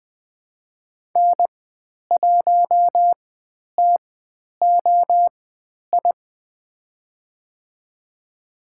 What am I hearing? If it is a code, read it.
Morse code, heard as a keyed tone, about 20 wpm, spaced as for 10 wpm, 700 Hz. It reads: N1TOI
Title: 095 Lesson 16g - Letter c - Call Signs 20wpm